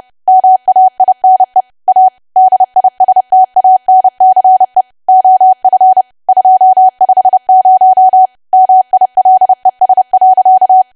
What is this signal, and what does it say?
Morse code, heard as a keyed tone, about 30 wpm, 750 Hz. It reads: MAINE A DISTANCE OF 250 MILES.
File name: cwa / W1AW-Level3 / 30WPM